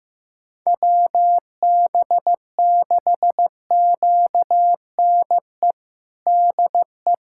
Morse code, heard as a keyed tone, about 15 wpm, 700 Hz. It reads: WB6QNE DE
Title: USA Callsign DE Callsign - 6 Characters - 15wpm